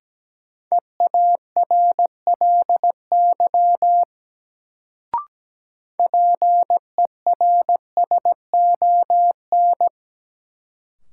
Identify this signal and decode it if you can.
Morse code, heard as a keyed tone, about 17 wpm, 700 Hz. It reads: EARLY PERSON
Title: Sentences from Top 400 Words - New Word Drill - 17wpm